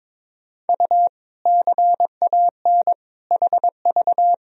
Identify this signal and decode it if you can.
Morse code, heard as a keyed tone, about 22 wpm, 700 Hz. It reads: U CAN HV
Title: Sentences from Top 100 Words and Abbreviations 22wpm